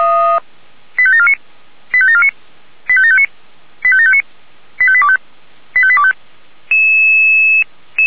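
Five-tone selective call (ZVEI), heard as quick runs of five tones, tones of about 80 ms. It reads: 85749 85749 85759 85759 85825 85825
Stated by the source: ZVEI five-tone call 85827, received on SDR.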